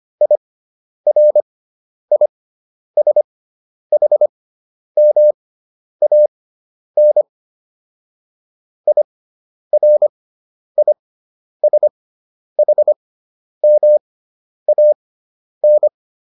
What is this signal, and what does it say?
Morse code, heard as a keyed tone, about 25 wpm, spaced as for 10 wpm, 600 Hz. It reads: IRISHMAN IRISHMAN